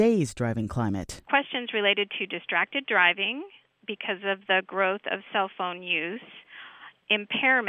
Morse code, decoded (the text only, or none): none